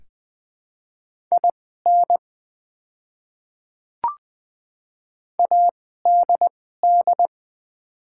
Morse code, heard as a keyed tone, about 20 wpm, spaced as for 14 wpm, 690 Hz. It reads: IN ADD
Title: 2x Character Spacing - Top 500 Words - 20wpm